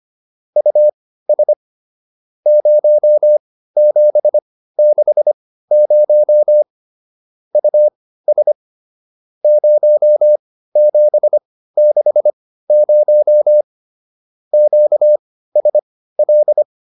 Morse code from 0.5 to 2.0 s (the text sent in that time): US